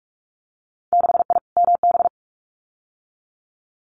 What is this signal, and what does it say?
Morse code, heard as a keyed tone, about 46 wpm, 700 Hz. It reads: 6I MB
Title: Winter Field Day Exchanges Encore 45wpm